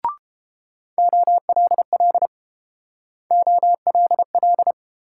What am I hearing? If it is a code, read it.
Morse code, heard as a keyed tone, about 32 wpm, 700 Hz. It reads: OLL OLL